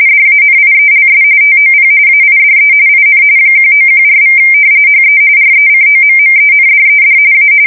RTTY, about 45.45 baud. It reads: DANIELS TOP AIDE IN 1914= HE RECOMMENDED A
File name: fsk170 / usnavy4